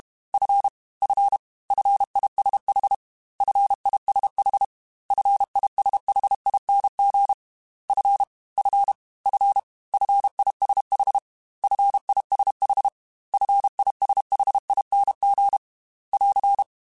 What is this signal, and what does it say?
Morse code, heard as a keyed tone, about 32 wpm, 800 Hz. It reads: F F FISH FISH FISHING F F F FISH FISH FISHING <AR>
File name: Fs / Single F mp3s / Single F Prefixes 3x2 Twice @30